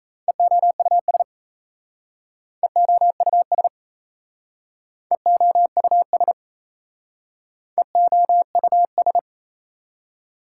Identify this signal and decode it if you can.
Morse code, heard as a keyed tone, about 35 wpm, 690 Hz. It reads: EOUS EOUS EOUS EOUS
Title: Speed-Racing Prefixes and Suffixes 28wpm